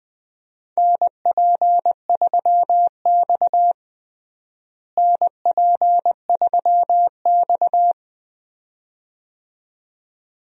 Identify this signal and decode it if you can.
Morse code, heard as a keyed tone, about 20 wpm, 700 Hz. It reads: NP3X NP3X
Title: USA Call Signs - 4 Characters - Repeated 2x - 20wpm